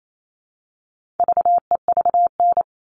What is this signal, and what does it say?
Morse code, heard as a keyed tone, about 28 wpm, 700 Hz. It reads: VEVN
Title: All Cut Numbers - 4-Digit 28wpm